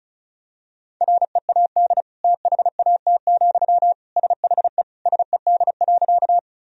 Morse code, heard as a keyed tone, about 34 wpm, 700 Hz. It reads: READ THAT, SHE SED.
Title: Sentences from Top 200 Words and Abbreviations 35wpm